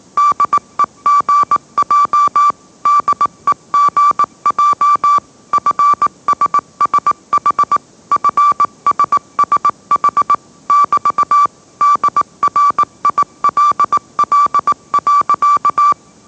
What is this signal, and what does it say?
Morse code, heard as a keyed tone, about 18 wpm, 1160 Hz. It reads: DEGJ DEGJ FSSH FSSH <BT> DRILL.
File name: mrx / mrx082 20wpm